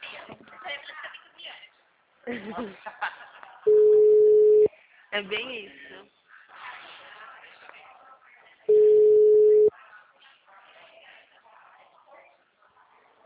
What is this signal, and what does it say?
Ringback tone (Europe, 425 Hz), with speech.